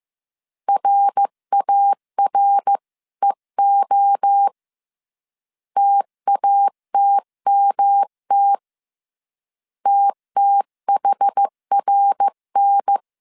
Morse code, heard as a keyed tone, about 15 wpm, 800 Hz. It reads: RAREO TATMT TTHRN
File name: CW / DET / 1 3 1500C S A 14wpm 2447